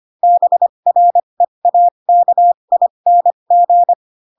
Morse code, heard as a keyed tone, about 25 wpm, spaced as for 22 wpm, 700 Hz. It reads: BREAKING